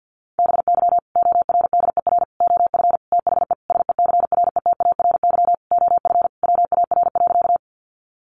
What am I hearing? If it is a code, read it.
Morse code, heard as a keyed tone, about 50 wpm, 700 Hz. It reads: BY ORDER OF THE SECRETARY OF WAR.
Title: Sentences from Top 1300 Words 50wpm